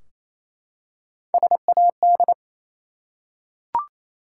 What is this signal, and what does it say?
Morse code, heard as a keyed tone, about 28 wpm, 700 Hz. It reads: SAD
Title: Three Letter Words - 28wpm